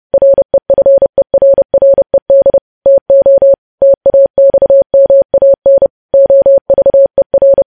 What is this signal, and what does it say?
Morse code, heard as a keyed tone, about 30 wpm, 550 Hz. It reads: REFERRED TO TAXMAN OVER
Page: news headlines in morse code at 30 wpm